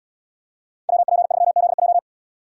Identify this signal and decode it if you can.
Morse code, heard as a keyed tone, about 74 wpm, 700 Hz. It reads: WP3ZJ